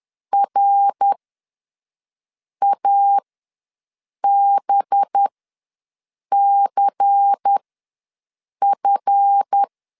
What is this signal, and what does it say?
Morse code, heard as a keyed tone, about 10 wpm, spaced as for 5 wpm, 800 Hz. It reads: RABCF
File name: LEZIONI / 9 2867 ws 10wpm 4215